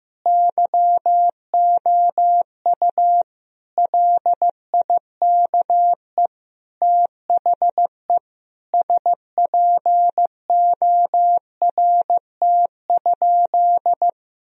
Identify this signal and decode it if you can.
Morse code, heard as a keyed tone, about 16 wpm, 700 Hz. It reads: YOU LIKE THE SPORT?
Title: Sentences from Top 1000 Words - Review Part 4 - 15wpm